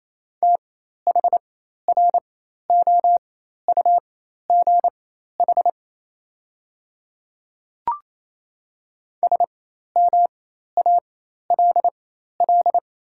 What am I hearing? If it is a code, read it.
Morse code, heard as a keyed tone, about 27 wpm, spaced as for 13 wpm, 700 Hz. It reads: THROUGH SMALL
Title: 4x Character Spacing - Top 200 Words - 28wpm